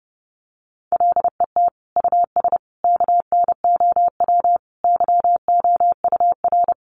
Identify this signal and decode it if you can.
Morse code, heard as a keyed tone, about 30 wpm, 700 Hz. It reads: LET US KNOW YOUR
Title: Sentences from Top 1400 Words 30wpm